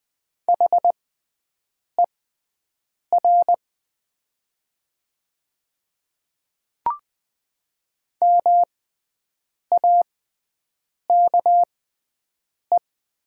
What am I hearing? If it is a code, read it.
Morse code, heard as a keyed tone, about 20 wpm, spaced as for 7 wpm, 700 Hz. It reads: HER MAKE
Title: 6x Character Spacing - Top 100 Words - 20wpm